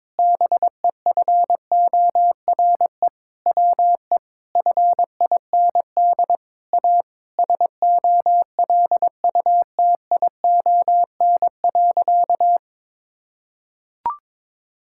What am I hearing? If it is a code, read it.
Morse code, heard as a keyed tone, about 22 wpm, 700 Hz. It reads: BEFORE WE FIND A SOLUTION.
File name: practice / 22wpm / Sentences with 8-Word Phrases - 22wpm